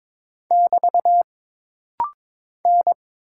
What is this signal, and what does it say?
Morse code, heard as a keyed tone, about 22 wpm, 700 Hz. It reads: <BT> N
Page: This rapid-fire timing is geared to push your ICR (Instant Character Recognition) skills.